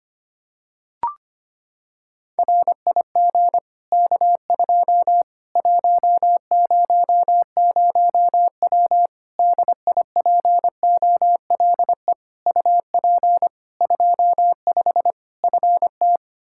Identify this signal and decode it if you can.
Morse code, heard as a keyed tone, about 25 wpm, 700 Hz. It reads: RIG K2 100W DIPOLE UP 25 FT
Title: Practice QSO - Exchange 2 - Info - 25wpm